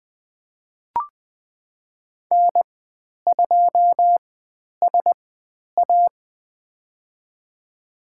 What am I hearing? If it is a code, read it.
Morse code, heard as a keyed tone, about 20 wpm, spaced as for 10 wpm, 700 Hz. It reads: N2SA
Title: 095 Lesson 16g - Letter c - Call Signs 20wpm